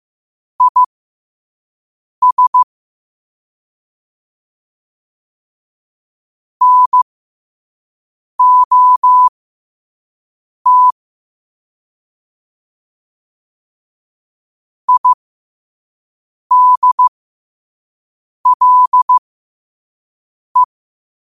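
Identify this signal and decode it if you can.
Morse code, heard as a keyed tone, about 15 wpm, spaced as for 5 wpm, 1000 Hz. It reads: IS NOT IDLE
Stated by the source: Morse at 5 wpm.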